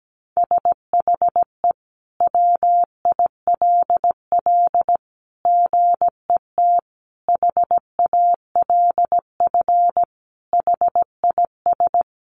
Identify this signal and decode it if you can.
Morse code, heard as a keyed tone, about 17 wpm, 700 Hz. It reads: SHE WILL GET HALF HIS
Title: Sentences from Top 1200 Words 17wpm